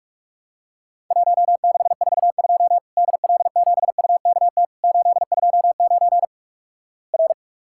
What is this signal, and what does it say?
Morse code, heard as a keyed tone, about 44 wpm, 700 Hz. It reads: 1642 DL7UKT 829 R
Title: Worked All Europe DX Contest - QTC Exchange - 45wpm